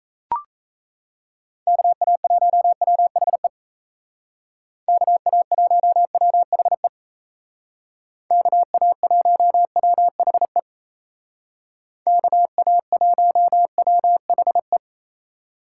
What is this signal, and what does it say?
Morse code, heard as a keyed tone, about 36 wpm, 700 Hz. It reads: KA1WHE KA1WHE KA1WHE KA1WHE